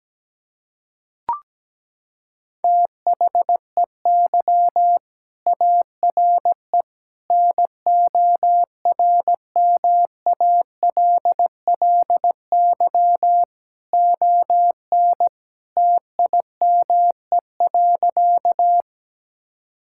Morse code, heard as a keyed tone, about 17 wpm, 700 Hz. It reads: THEY ARE NORMALLY ON TIME.
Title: Sentences from Top 1000 Words - Review Part 6 - 17wpm